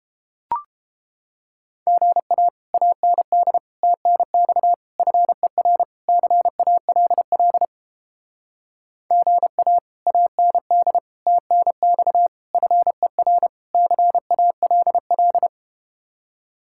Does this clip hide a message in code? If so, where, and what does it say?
Yes: Morse code from 1.9 to 15.5 s, heard as a keyed tone, about 31 wpm, 700 Hz. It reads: GA AND TNX FER CALL GA AND TNX FER CALL